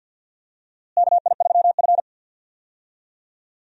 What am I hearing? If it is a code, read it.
Morse code, heard as a keyed tone, about 52 wpm, 700 Hz. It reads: KI3F
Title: USA Call Signs - 4 Characters - Repeated 2x - 50wpm